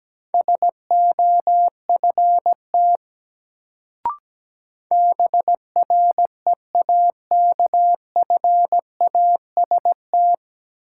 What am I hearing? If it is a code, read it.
Morse code, heard as a keyed tone, about 17 wpm, 700 Hz. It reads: SOFT BREAKFAST
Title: Sentences from Top 1500 Words - New Word Drill - 17wpm